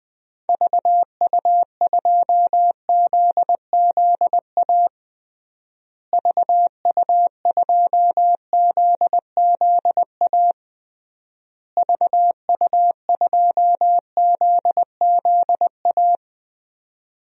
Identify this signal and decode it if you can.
Morse code, heard as a keyed tone, about 20 wpm, 700 Hz. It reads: VU2ZZA VU2ZZA VU2ZZA
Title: DXCC-100 Call Signs - 6 Characters - Repeated 3x - 20wpm